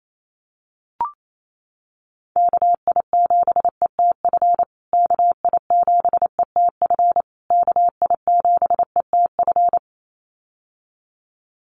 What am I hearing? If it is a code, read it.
Morse code, heard as a keyed tone, about 28 wpm, 700 Hz. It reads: KI7ETF KI7ETF KI7ETF
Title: USA Call Signs Repeated 3x 28wpm